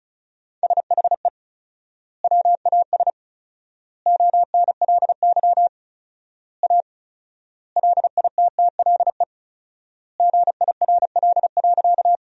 Morse code, heard as a keyed tone, about 34 wpm, 700 Hz. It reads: SHE WAS ONLY A LITTLE GIRL.
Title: 4x Word-Spacing - Sentences from Top 400 Words - 35wpm